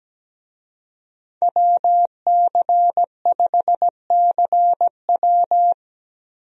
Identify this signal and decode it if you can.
Morse code, heard as a keyed tone, about 17 wpm, 700 Hz. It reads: WC5CW
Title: USA Call Signs - 5 Characters - Repeated 3x - 17wpm